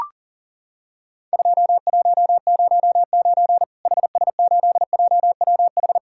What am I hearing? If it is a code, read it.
Morse code, heard as a keyed tone, about 40 wpm, 700 Hz. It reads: 2109 HS8JWH
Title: Worked All Europe DX Contest - QTC Exchange - 40wpm